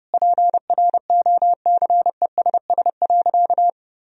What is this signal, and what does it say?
Morse code, heard as a keyed tone, about 30 wpm, 700 Hz. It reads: PROCESS.
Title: Sentences from Top 1000 Words - Review Part 3 - 30wpm